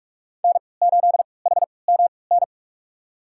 Morse code, heard as a keyed tone, about 46 wpm, spaced as for 26 wpm, 700 Hz. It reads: N8HMN
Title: USA Call Signs - 5 Characters - 3x Character Spacing - 45wpm